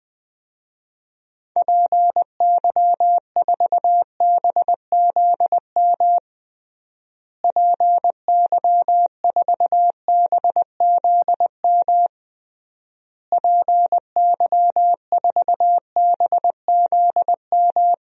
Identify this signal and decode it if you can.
Morse code, heard as a keyed tone, about 20 wpm, 700 Hz. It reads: PY4BZM PY4BZM PY4BZM
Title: DXCC-100 Call Signs - 6 Characters - Repeated 3x - 20wpm